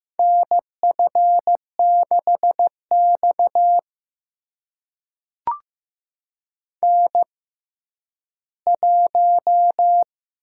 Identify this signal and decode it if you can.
Morse code, heard as a keyed tone, about 15 wpm, spaced as for 10 wpm, 700 Hz. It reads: NF6X N1
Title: USA Call Signs - 4 Characters - 6x Character Spacing - 15wpm